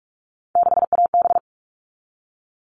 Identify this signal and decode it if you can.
Morse code, heard as a keyed tone, about 44 wpm, 700 Hz. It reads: 6AB
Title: All Cut Numbers - 3-Digit 45wpm